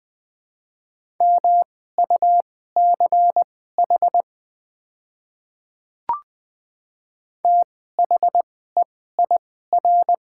Morse code, heard as a keyed tone, about 20 wpm, spaced as for 14 wpm, 700 Hz. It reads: MUCH THEIR
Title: 2x Character Spacing - Top 200 Words - 20wpm